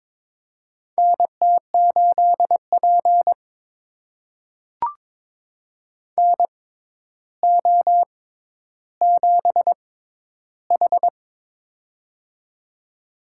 Morse code, heard as a keyed tone, about 22 wpm, spaced as for 11 wpm, 700 Hz. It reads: NT8P NO7H
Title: USA Call Signs - 4 Characters - 6x Character Spacing - 22wpm